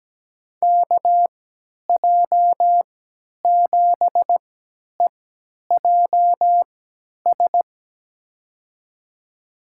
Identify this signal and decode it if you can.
Morse code, heard as a keyed tone, about 17 wpm, spaced as for 10 wpm, 700 Hz. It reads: KJ7EJS
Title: USA Call Signs - 6 Characters - 3x Character Spacing - 17wpm